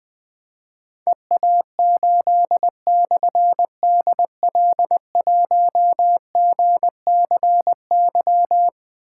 Morse code, heard as a keyed tone, about 21 wpm, 700 Hz. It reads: EA8/DL1GCY
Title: Difficult Call Signs 20wpm